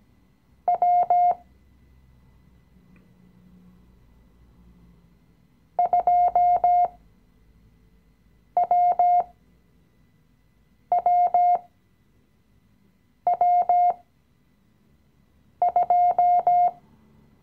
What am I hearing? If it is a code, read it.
Morse code, heard as a keyed tone, about 17 wpm, spaced as for 5 wpm, 710 Hz. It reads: W 2WWW2
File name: CW-Online-Kurs / Anfänger / Lektion 017